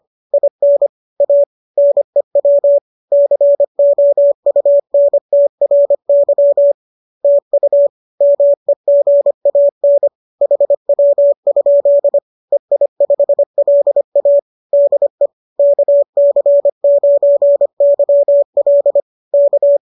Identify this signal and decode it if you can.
Morse code, heard as a keyed tone, about 25 wpm, 570 Hz. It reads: IN A NEW COUNTRY TU MEGAN HW? EI5LA DE KC9YL K